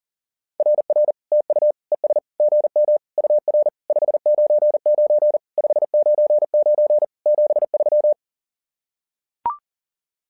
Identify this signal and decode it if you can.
Morse code, heard as a keyed tone, about 40 wpm, 600 Hz. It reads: RR TU ES GM UR 599 599 73
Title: SOTA QSO Practice 40wpm